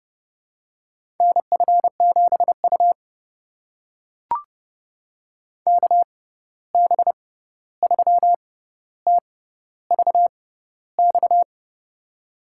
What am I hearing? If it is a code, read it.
Morse code, heard as a keyed tone, about 30 wpm, spaced as for 14 wpm, 700 Hz. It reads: NF7U KB3TVX
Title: USA Call Signs - 6x Character Spacing - 30wpm